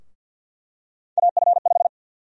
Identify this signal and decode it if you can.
Morse code, heard as a keyed tone, about 50 wpm, 700 Hz. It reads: AF5
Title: USA Call Sign Prefixes - 50wpm